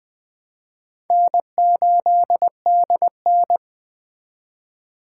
Morse code, heard as a keyed tone, about 21 wpm, 700 Hz. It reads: N8DN
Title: Speed Racing - USA Call Signs 17wpm